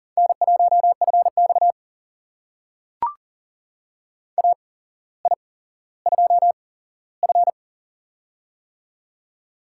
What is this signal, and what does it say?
Morse code, heard as a keyed tone, about 40 wpm, spaced as for 17 wpm, 700 Hz. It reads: N1FX AI2F
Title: USA Call Signs - 4 Characters - 8x Character Spacing - 40wpm